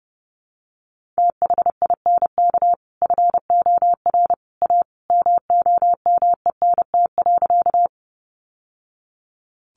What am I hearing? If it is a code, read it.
Morse code, heard as a keyed tone, about 30 wpm, 700 Hz. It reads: THINK FOR A MOMENT.
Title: Sentences from Top 1000 Words 30wpm